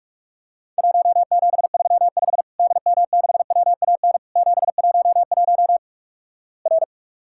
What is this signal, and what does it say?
Morse code, heard as a keyed tone, about 48 wpm, 700 Hz. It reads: 1735 DM6WAN 711 R
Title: Worked All Europe DX Contest - QTC Exchange - 45wpm